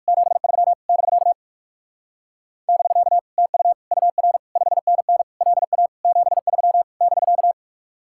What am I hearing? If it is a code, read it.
Morse code, heard as a keyed tone, about 56 wpm, 700 Hz. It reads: 73 <BK> <BK> TU UR 5NN LA 73 <BK>
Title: Speed-Racing Common QSO Phrases - 45wpm